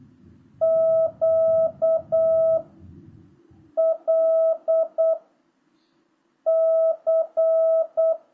Morse code, heard as a keyed tone, about 8 wpm, 650 Hz. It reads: Q L C